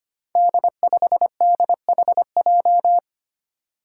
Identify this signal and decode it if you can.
Morse code, heard as a keyed tone, about 25 wpm, 700 Hz. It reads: D5DHJ
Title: Five Challenging Letter-Number Combinations 25wpm